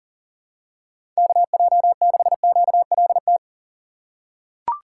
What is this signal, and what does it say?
Morse code, heard as a keyed tone, about 40 wpm, 700 Hz. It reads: KJ6QLT
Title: USA Call Signs - 6 Characters - 6x Character Spacing - 40wpm